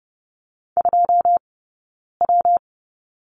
Morse code, heard as a keyed tone, about 30 wpm, 700 Hz. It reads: 2 W
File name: Lesson 20d - Number 3 - Two Characters 30wpm